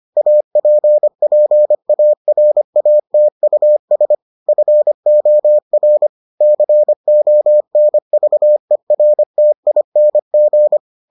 Morse code, heard as a keyed tone, about 25 wpm, 600 Hz. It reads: APPARATUS FOR CONVERTING